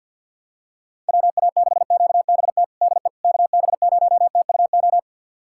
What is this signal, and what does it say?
Morse code, heard as a keyed tone, about 52 wpm, 700 Hz. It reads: WA6QBT DE KB0TUO
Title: USA Callsign DE Callsign 50wpm